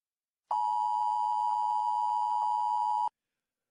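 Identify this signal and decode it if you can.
Emergency Alert System (SAME) tones (the 853 and 960 Hz attention tone). It is heard from 0.5 to 3.1 s.
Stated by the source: Fictional emergency broadcast in the audio drama The After Disaster Broadcast, voiced by Wesley Whitaker.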